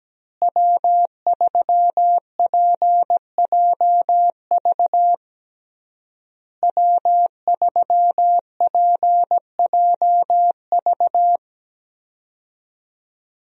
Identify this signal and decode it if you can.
Morse code, heard as a keyed tone, about 17 wpm, 700 Hz. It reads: W3PJV W3PJV
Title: USA Call Signs - 5 Characters - Repeated 3x - 17wpm